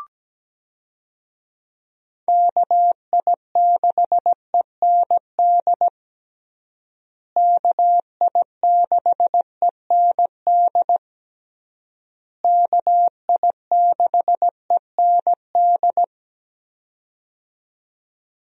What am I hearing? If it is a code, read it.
Morse code, heard as a keyed tone, about 17 wpm, 700 Hz. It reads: KI6END KI6END KI6END
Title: USA Call Signs - 6 Characters - Repeated 3x - 17wpm